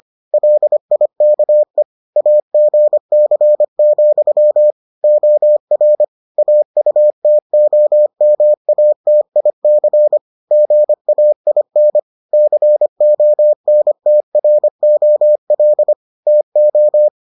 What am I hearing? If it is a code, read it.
Morse code, heard as a keyed tone, about 25 wpm, 600 Hz. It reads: LIKE AGC, OR AUTOMATIC GAIN CONTROL TO